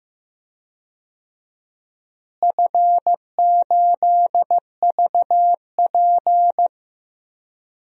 Morse code, heard as a keyed tone, about 15 wpm, 700 Hz. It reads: F8VP
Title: DXCC-100 Call Signs - 4 Characters - Repeated 3x - 15wpm